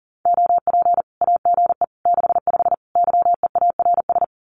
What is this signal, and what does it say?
Morse code, heard as a keyed tone, about 39 wpm, 700 Hz. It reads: OP AGE 65 YEARS
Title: Common QSO Phrases 40wpm